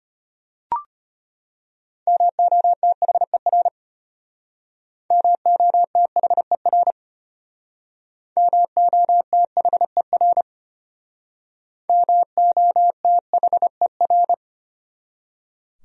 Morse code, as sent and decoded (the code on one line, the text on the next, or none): -- --- - .... .-. / -- --- - .... .-. / -- --- - .... .-. / -- --- - .... .-.
MOTHER MOTHER MOTHER MOTHER